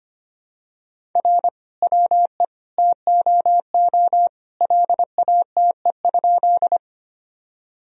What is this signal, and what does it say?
Morse code, heard as a keyed tone, about 25 wpm, 700 Hz. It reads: R WE TOO LATE?
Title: Sentences from Top 300 Words and Abbreviations 25wpm